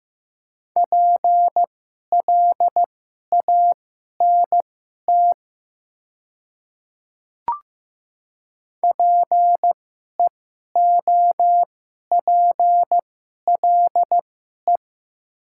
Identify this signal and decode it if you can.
Morse code, heard as a keyed tone, about 15 wpm, spaced as for 11 wpm, 700 Hz. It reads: PLANT PEOPLE
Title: 2x Character Spacing - Top 300 Words - 15wpm